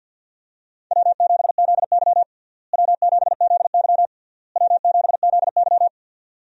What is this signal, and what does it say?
Morse code, heard as a keyed tone, about 51 wpm, 700 Hz. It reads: W7ZY W7ZY W7ZY
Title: USA Call Signs - 4 Characters - Repeated 3x - 50wpm